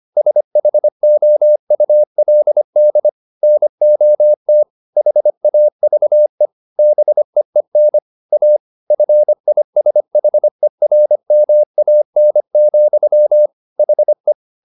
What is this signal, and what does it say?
Morse code, heard as a keyed tone, about 25 wpm, 600 Hz. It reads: SHOULD NOT HAVE BEEN A FISHERMAN, HE
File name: the old man and the sea part 1 25wpm